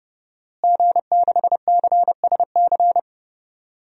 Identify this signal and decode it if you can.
Morse code, heard as a keyed tone, about 31 wpm, 700 Hz. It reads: G6CSC
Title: DXCC-100 Call Signs - 5 Characters - Repeated 3x - 30wpm